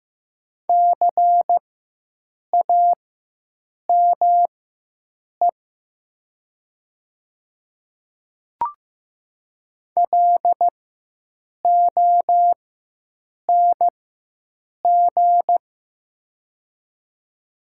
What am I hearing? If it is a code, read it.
Morse code, heard as a keyed tone, about 15 wpm, spaced as for 7 wpm, 700 Hz. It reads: CAME LONG
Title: 4x Character Spacing - Top 200 Words - 15wpm